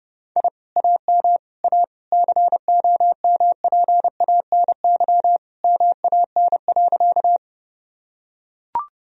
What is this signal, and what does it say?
Morse code, heard as a keyed tone, about 30 wpm, 700 Hz. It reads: I AM A COMPANY MAN.
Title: Sentences from Top 1000 Words - Review Part 3 - 30wpm